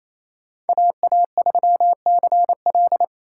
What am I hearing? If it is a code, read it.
Morse code, heard as a keyed tone, about 28 wpm, 700 Hz. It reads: AA3CL
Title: USA Call Signs - 4x Character Spacing - 28wpm